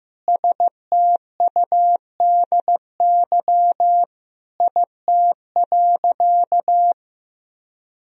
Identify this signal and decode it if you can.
Morse code, heard as a keyed tone, about 14 wpm, 700 Hz. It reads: STUDY IT.
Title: Sentences from Top 1000 Words - Review Part 1 - 15wpm